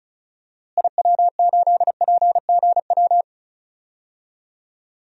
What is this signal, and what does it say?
Morse code, heard as a keyed tone, about 36 wpm, 700 Hz. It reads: IW8PGW
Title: DXCC-100 Call Signs - 6 Characters - Repeated 2x - 35wpm